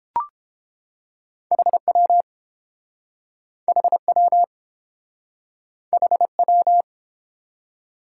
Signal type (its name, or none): Morse code